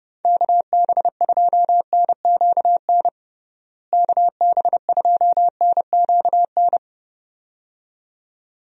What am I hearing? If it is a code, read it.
Morse code, heard as a keyed tone, about 29 wpm, 700 Hz. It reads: KB2NQN KB2NQN
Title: USA Call Signs - 6 Characters - Repeated 2x - 30wpm